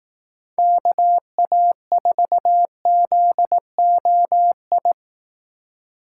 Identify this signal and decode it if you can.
Morse code, heard as a keyed tone, about 18 wpm, 700 Hz. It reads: KA4ZOI
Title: Speed Racing - USA 2x3 Call Signs 15wpm